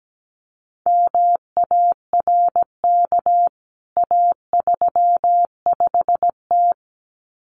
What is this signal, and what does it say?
Morse code, heard as a keyed tone, about 17 wpm, 700 Hz. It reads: MARK A35T
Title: CWT Exchange 17wpm